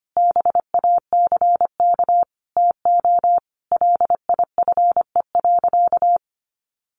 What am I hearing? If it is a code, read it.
Morse code, heard as a keyed tone, about 25 wpm, 700 Hz. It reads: BACK TO LIFE.
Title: Sentences from Top 600 Words 25wpm